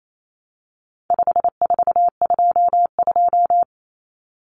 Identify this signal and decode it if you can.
Morse code, heard as a keyed tone, about 28 wpm, 700 Hz. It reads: 5422
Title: Numbers - 4-Digit 28wpm